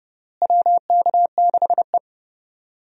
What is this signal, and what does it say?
Morse code, heard as a keyed tone, about 29 wpm, 700 Hz. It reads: WK6E
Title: USA Call Signs - 4 Characters - Repeated 3x - 30wpm